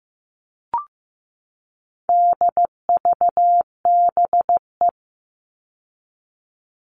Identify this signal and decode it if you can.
Morse code, heard as a keyed tone, about 15 wpm, 700 Hz. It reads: DVBE
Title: All Cut Numbers - 4-Digit 15wpm